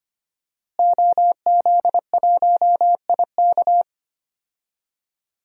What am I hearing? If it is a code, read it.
Morse code, heard as a keyed tone, about 25 wpm, 700 Hz. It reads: OZ1IK
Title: DXCC-100 Call Signs - 5 Characters - Repeated 2x - 25wpm